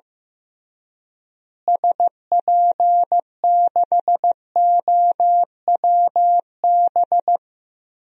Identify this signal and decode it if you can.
Morse code, heard as a keyed tone, about 15 wpm, 700 Hz. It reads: SP6OWB
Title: DXCC-100 Call Signs - 6 Characters - Repeated 2x - 15wpm